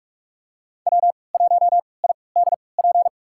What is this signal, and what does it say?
Morse code, heard as a keyed tone, about 44 wpm, spaced as for 25 wpm, 700 Hz. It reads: W1IDP
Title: USA Call Signs - 5 Characters - 3x Character Spacing - 45wpm